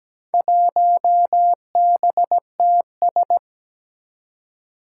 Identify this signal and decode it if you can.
Morse code, heard as a keyed tone, about 17 wpm, 700 Hz. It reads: 1BTS
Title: Four Challenging Letter-Number Combinations 17wpm